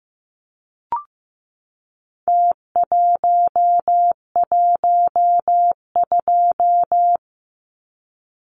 Morse code, heard as a keyed tone, about 15 wpm, 700 Hz. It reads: T112
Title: Numbers - Contest Style - 4-Digit 15wpm